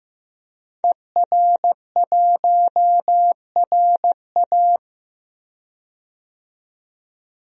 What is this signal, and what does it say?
Morse code, heard as a keyed tone, about 16 wpm, 700 Hz. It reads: ER1RA
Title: DXCC-100 Call Signs - 5 Characters - 15wpm